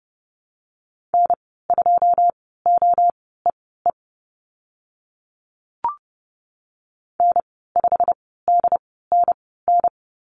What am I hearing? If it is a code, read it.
Morse code, heard as a keyed tone, about 30 wpm, spaced as for 17 wpm, 700 Hz. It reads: N2OEE N5DNN
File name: Lesson 15g - Number 5 - Call Signs 30wpm